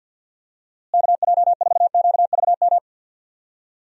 Morse code, heard as a keyed tone, about 46 wpm, 700 Hz. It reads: KJ4QVM
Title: USA Call Signs - 6 Characters - Repeated 2x - 50wpm